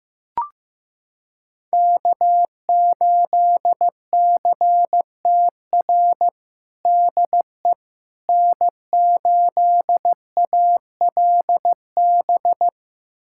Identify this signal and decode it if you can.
Morse code, heard as a keyed tone, about 15 wpm, 700 Hz. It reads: K8CTR DE N8ALB
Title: USA Callsign DE Callsign - 5 Characters - 15wpm